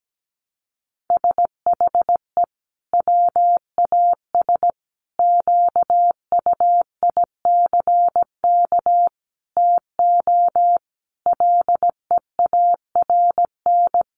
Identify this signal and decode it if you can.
Morse code, heard as a keyed tone, about 17 wpm, 700 Hz. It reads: SHE WAS QUICK TO LEARN
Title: Sentences from Top 1200 Words 17wpm